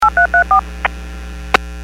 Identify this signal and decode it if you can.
DTMF, heard as telephone touch tones, tones of about 110 ms. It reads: 8334
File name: WPS / 01301 14-41 WPS 451.25 WPS